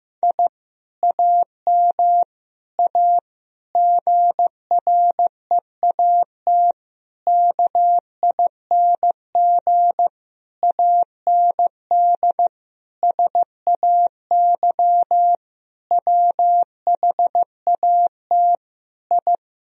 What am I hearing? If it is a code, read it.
Morse code, heard as a keyed tone, about 15 wpm, 700 Hz. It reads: I AM A GREAT KING AND SAY WHAT I
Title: Sentences from Top 1000 Words - Review Part 4 - 15wpm